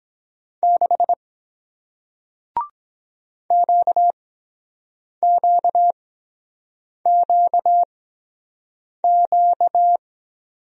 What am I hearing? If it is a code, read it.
Morse code, heard as a keyed tone, about 23 wpm, spaced as for 7 wpm, 700 Hz. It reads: 6 QQQQ